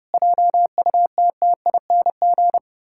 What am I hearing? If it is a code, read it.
Morse code, heard as a keyed tone, about 30 wpm, 700 Hz. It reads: JUTTING
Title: Pangram Challenge v2 - 30wpm